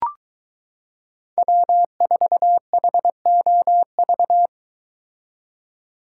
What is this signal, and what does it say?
Morse code, heard as a keyed tone, about 22 wpm, 700 Hz. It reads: W4HOV